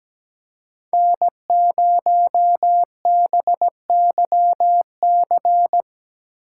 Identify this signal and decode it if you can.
Morse code, heard as a keyed tone, about 19 wpm, 700 Hz. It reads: N0BYC
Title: Speed Racing - USA 1x3 Call Signs 17wpm